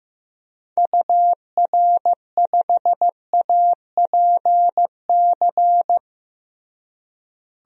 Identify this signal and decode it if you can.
Morse code, heard as a keyed tone, about 15 wpm, 700 Hz. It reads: UR5APC
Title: DXCC-100 Call Signs - 6 Characters - Repeated 2x - 15wpm